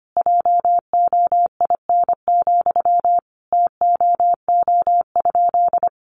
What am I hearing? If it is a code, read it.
Morse code, heard as a keyed tone, about 25 wpm, 700 Hz. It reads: JOIN, TOO?
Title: Sentences from Top 1400 Words 25wpm